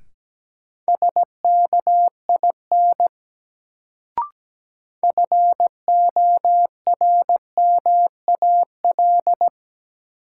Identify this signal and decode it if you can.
Morse code, heard as a keyed tone, about 17 wpm, 700 Hz. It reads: SKIN FORMAL
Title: Sentences from Top 1300 Words - New Word Drill - 17wpm